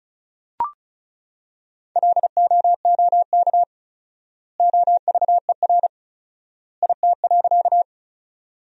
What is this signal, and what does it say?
Morse code, heard as a keyed tone, about 34 wpm, 700 Hz. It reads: LOOK OVER IT.
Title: 4x Word-Spacing - Sentences from Top 100 Words - 35wpm